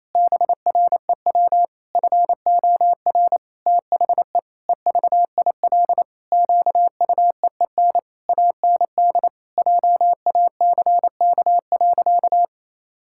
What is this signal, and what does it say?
Morse code, heard as a keyed tone, about 28 wpm, 700 Hz. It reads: BREW FOR THE EVIL QUEEN AND JACK.
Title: Pangram Challenge v2 - 28wpm